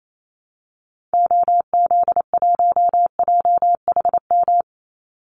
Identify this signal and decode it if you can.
Morse code, heard as a keyed tone, about 28 wpm, 700 Hz. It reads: OZ1JHM
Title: Call Signs Repeated 2x - 28wpm